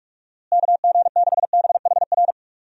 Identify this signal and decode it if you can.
Morse code, heard as a keyed tone, about 43 wpm, 700 Hz. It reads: KG6BHR
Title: USA Call Signs - 6 Characters - Repeated 3x - 45wpm